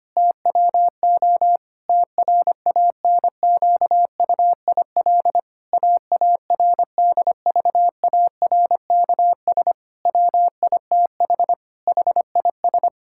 Morse code, heard as a keyed tone, about 25 wpm, 700 Hz. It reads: TWO TRANQUIL AARDVARKS WITH HIS
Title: Pangram Challenge v2 - 25wpm